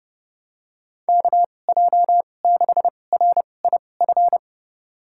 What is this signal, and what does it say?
Morse code, heard as a keyed tone, about 32 wpm, spaced as for 22 wpm, 700 Hz. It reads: KJ6RIF
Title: USA Call Signs - 6 Characters - 2x Character Spacing - 30wpm